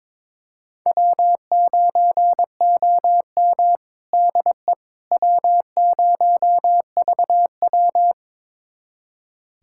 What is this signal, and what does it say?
Morse code, heard as a keyed tone, about 22 wpm, 700 Hz. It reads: W9OM DE W0VW
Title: USA Callsign DE Callsign - 4 Characters - 22wpm